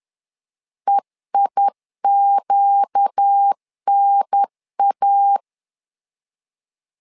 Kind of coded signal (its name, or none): Morse code